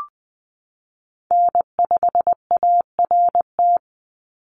Morse code, heard as a keyed tone, about 20 wpm, 700 Hz. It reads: N5ART
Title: USA Call Signs 20wpm